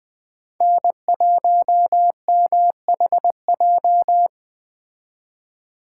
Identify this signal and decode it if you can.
Morse code, heard as a keyed tone, about 21 wpm, 700 Hz. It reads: N1MHJ